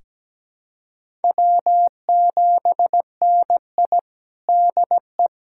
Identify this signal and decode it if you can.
Morse code, heard as a keyed tone, about 17 wpm, 700 Hz. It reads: W7NI DE
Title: USA Callsign DE Callsign - 4 Characters - 17wpm